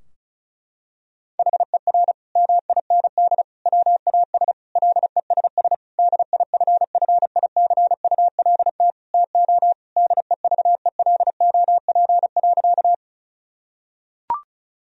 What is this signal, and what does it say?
Morse code, heard as a keyed tone, about 36 wpm, 700 Hz. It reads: HER MIND WAS LESS DIFFICULT TO DEVELOP.
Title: Sentences from Top 1000 Words - Review Part 2 - 35wpm